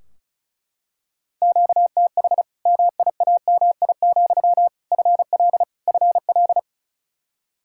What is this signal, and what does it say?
Morse code, heard as a keyed tone, about 34 wpm, 700 Hz. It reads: QTH MIAMI, FL FL
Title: Practice QSO - Exchange 1 - Beginning - 35wpm